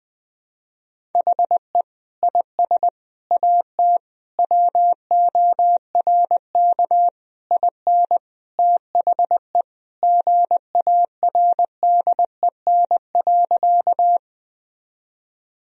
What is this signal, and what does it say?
Morse code, heard as a keyed tone, about 21 wpm, 700 Hz. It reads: HE IS AT WORK IN THE GARDEN.
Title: Sentences from Top 1000 Words - Review Part 4 - 20wpm